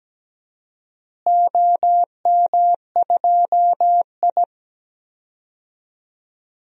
Morse code, heard as a keyed tone, about 17 wpm, 700 Hz. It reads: OM2I
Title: DXCC-100 Call Signs - 4 Characters - Repeated 3x - 17wpm